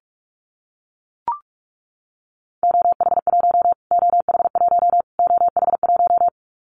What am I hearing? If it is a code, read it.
Morse code, heard as a keyed tone, about 44 wpm, 700 Hz. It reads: OH1 OH1 OH1
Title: Callsign Prefixes for Top 25 DXCC - Repeated 3x - 45wpm